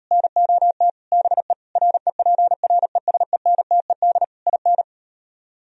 Morse code, heard as a keyed tone, about 38 wpm, 700 Hz. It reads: NOT BE REPRESENTED IN